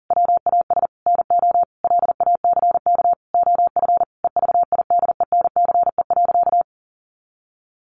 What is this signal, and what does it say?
Morse code, heard as a keyed tone, about 40 wpm, 700 Hz. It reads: WAS NO LACK OF EVIDENCE.
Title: Sentences from Top 1300 Words 40wpm